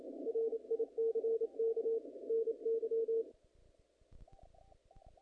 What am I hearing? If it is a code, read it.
Morse code, heard as a keyed tone, about 28 wpm, 450 Hz. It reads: RICK NY
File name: contests / naqp-cw-k5tr-op-k5pi- -0100